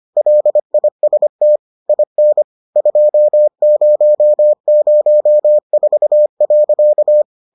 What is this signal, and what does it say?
Morse code, heard as a keyed tone, about 25 wpm, 600 Hz. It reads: LIST IN 2004.